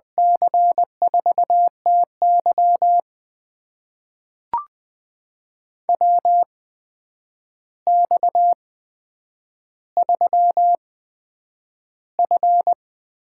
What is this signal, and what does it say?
Morse code, heard as a keyed tone, about 20 wpm, spaced as for 9 wpm, 700 Hz. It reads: C4TY WX3F
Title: USA Call Signs - 8x Character Spacing - 20wpm